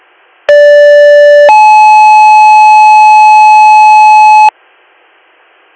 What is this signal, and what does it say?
Two-tone page, heard as a one-second tone, then a three-second tone, about 580 then 860 Hz. It is heard from 0.5 to 4.5 s.